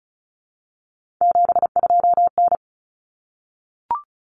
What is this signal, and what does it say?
Morse code, heard as a keyed tone, about 34 wpm, 700 Hz. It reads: 72N
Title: Common Cut Numbers - 3-Digit 35wpm